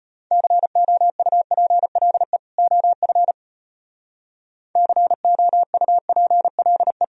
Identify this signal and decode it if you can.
Morse code, heard as a keyed tone, about 37 wpm, 700 Hz. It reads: COUPLE OF COUPLE